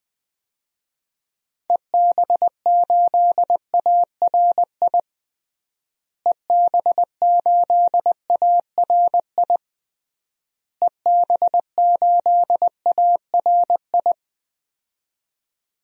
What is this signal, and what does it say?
Morse code, heard as a keyed tone, about 20 wpm, 700 Hz. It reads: EB8ARI EB8ARI EB8ARI
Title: DXCC-100 Call Signs - 6 Characters - Repeated 3x - 20wpm